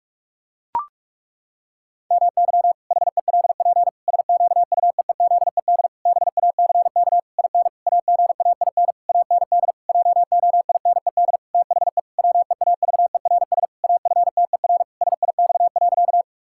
Morse code, heard as a keyed tone, about 41 wpm, 700 Hz. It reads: MY HELP SQUEEZED BACK IN AGAIN AND JOINED THE WEAVERS AFTER SIX.
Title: Pangram Challenge v2 - 45wpm